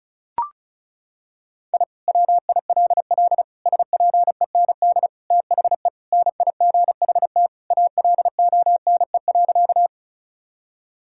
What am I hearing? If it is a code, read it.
Morse code, heard as a keyed tone, about 36 wpm, 700 Hz. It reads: I WILL SPEND THE NIGHT ALONE.
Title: Sentences from Top 1000 Words - Review Part 5 - 35wpm